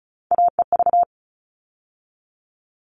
Morse code, heard as a keyed tone, about 34 wpm, 700 Hz. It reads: AEV
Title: All Cut Numbers - 3-Digit 35wpm